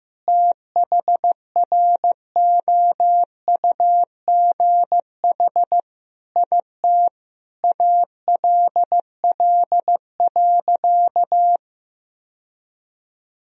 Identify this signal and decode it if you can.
Morse code, heard as a keyed tone, about 15 wpm, 700 Hz. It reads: THROUGH IT ALL.
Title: Sentences from Top 1000 Words - Review Part 1 - 15wpm